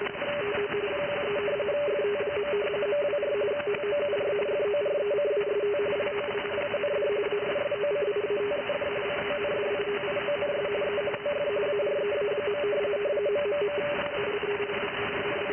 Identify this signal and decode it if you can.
Two-tone teleprinter signal, about 170 Hz shift.
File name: IC-7100